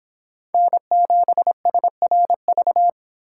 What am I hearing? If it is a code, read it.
Morse code, heard as a keyed tone, about 24 wpm, 700 Hz. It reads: N7SRV